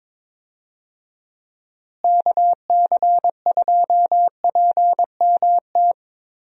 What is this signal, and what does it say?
Morse code, heard as a keyed tone, about 21 wpm, 700 Hz. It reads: KC2PMT
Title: USA Call Signs - 6 Characters - Repeated 2x - 22wpm